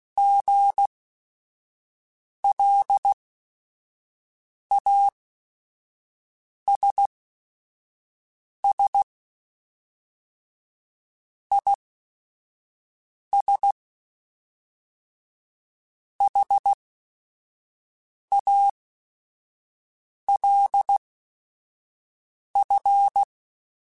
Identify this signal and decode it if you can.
Morse code, heard as a keyed tone, about 16 wpm, spaced as for 5 wpm, 800 Hz. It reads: GLASS IS HALF
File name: Chuckles / Chuckles-mp3s / Glass Half Full @5